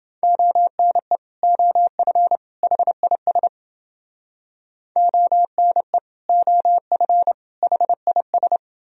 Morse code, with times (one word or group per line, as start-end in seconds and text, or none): 0.2-1.2 ONE
1.4-2.4 OF
2.6-3.5 HIS
5.0-6.0 ONE
6.3-7.3 OF
7.6-8.6 HIS